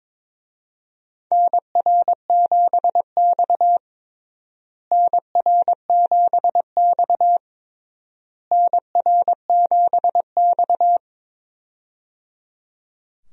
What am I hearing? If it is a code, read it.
Morse code, heard as a keyed tone, about 22 wpm, 700 Hz. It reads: NR7X NR7X NR7X
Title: USA Call Signs - 4 Characters - Repeated 3x - 22wpm